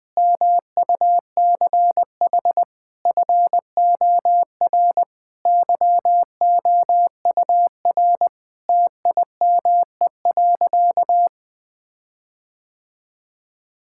Morse code, heard as a keyed tone, about 21 wpm, 700 Hz. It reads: MUCH FOR YOUR TIME.
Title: Sentences with 5-Word Phrases - 20wpm